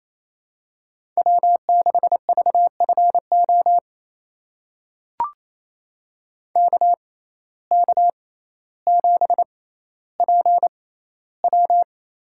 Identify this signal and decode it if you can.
Morse code, heard as a keyed tone, about 28 wpm, spaced as for 14 wpm, 700 Hz. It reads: W6VFO KK7PW
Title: USA Call Signs - 5 Characters - 6x Character Spacing - 28wpm